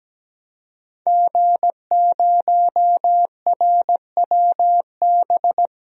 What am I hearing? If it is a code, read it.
Morse code, heard as a keyed tone, about 17 wpm, 700 Hz. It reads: G0RWB
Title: DXCC-100 Call Signs - 5 Characters - 17wpm